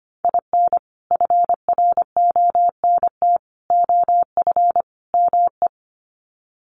Morse code, heard as a keyed tone, about 25 wpm, 700 Hz. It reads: IN FRONT OF ME
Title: Sets of 4 Words 25wpm